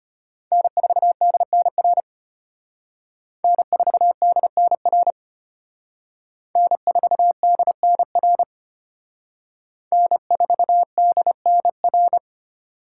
Morse code, heard as a keyed tone, about 32 wpm, 700 Hz. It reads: N4DNR N4DNR N4DNR N4DNR